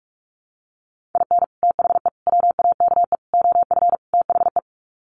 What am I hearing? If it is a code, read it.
Morse code, heard as a keyed tone, about 44 wpm, 700 Hz. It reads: IN THE WAKE OF THE
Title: Sets of 5 Words 45wpm